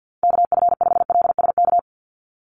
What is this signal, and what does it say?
Morse code, heard as a keyed tone, about 48 wpm, 700 Hz. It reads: KF5LSK
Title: USA Call Signs 50wpm